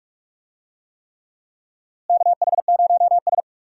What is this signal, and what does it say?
Morse code, heard as a keyed tone, about 46 wpm, 700 Hz. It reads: KH0S
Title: USA Call Signs - 4 Characters - Repeated 3x - 45wpm